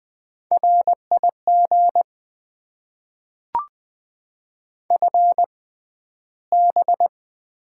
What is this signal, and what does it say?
Morse code, heard as a keyed tone, about 20 wpm, spaced as for 12 wpm, 700 Hz. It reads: RIG FB
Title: Top 90 QSO Elements - 6x Character Spacing - 20wpm